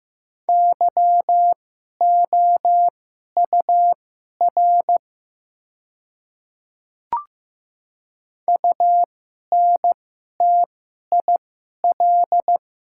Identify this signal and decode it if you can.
Morse code, heard as a keyed tone, about 15 wpm, spaced as for 11 wpm, 700 Hz. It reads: YOUR UNTIL
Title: 2x Character Spacing - Top 300 Words - 15wpm